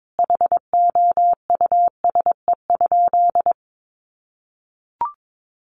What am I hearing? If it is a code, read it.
Morse code, heard as a keyed tone, about 22 wpm, 700 Hz. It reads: HOUSE?
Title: Sentences from Top 600 Words 22wpm